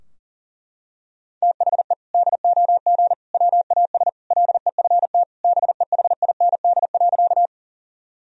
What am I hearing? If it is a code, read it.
Morse code, heard as a keyed tone, about 40 wpm, 700 Hz. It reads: THE DOG WAS LEFT BEHIND.
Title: Sentences from Top 1000 Words - Review Part 2 - 40wpm